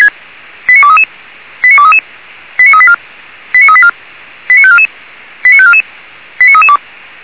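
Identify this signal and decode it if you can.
Five-tone selective call (ZVEI), heard as quick runs of five tones, tones of about 70 ms. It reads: 89233 89233 89384 89384 89544 89544 89332
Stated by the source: ZVEI five-tone selective call received on SDR.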